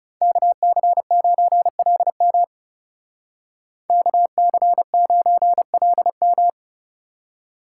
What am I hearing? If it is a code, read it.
Morse code, heard as a keyed tone, about 34 wpm, 700 Hz. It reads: KC9LM KC9LM
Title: Speed Racing - USA 2x2 Call Signs 30wpm